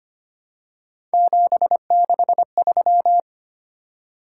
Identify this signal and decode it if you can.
Morse code, heard as a keyed tone, about 25 wpm, 700 Hz. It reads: 763
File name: Three Letter-Number Combinations - 25wpm